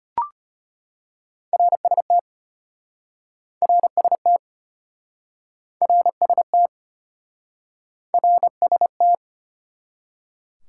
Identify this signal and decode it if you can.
Morse code, heard as a keyed tone, about 33 wpm, 700 Hz. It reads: RST RST RST RST